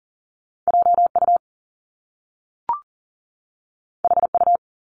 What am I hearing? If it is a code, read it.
Morse code, heard as a keyed tone, about 40 wpm, 700 Hz. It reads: JU HU
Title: Top Two Letter Combinations in Words 40wpm